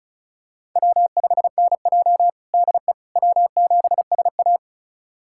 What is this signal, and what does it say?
Morse code, heard as a keyed tone, about 34 wpm, 700 Hz. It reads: W5NJ DE W7SA
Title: USA Callsign DE Callsign - 4 Characters - 35wpm